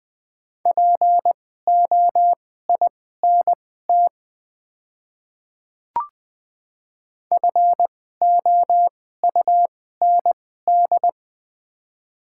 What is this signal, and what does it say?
Morse code, heard as a keyed tone, about 20 wpm, spaced as for 15 wpm, 690 Hz. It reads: POINT FOUND
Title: 2x Character Spacing - Top 500 Words - 20wpm